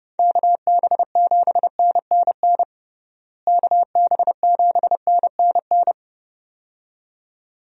Morse code, heard as a keyed tone, about 30 wpm, 700 Hz. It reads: KB7NNN KB7NNN
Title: USA Call Signs - 6 Characters - Repeated 2x - 30wpm